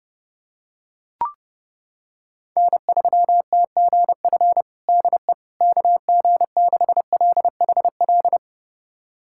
Morse code, heard as a keyed tone, about 30 wpm, 700 Hz. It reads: N3TGF DE KG6LHL
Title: USA Callsign DE Callsign 30wpm